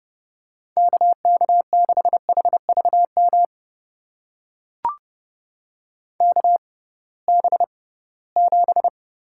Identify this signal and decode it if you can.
Morse code, heard as a keyed tone, about 32 wpm, spaced as for 20 wpm, 700 Hz. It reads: KK6HVM KB7
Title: USA Call Signs - 6 Characters - 6x Character Spacing - 30wpm